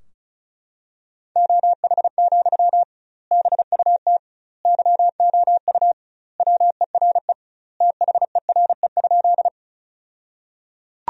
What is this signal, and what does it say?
Morse code, heard as a keyed tone, about 36 wpm, 700 Hz. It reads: OH, BUT YOU WERE THERE?
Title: 2x Word-Spacing - Sentences from Top 500 Words - 35wpm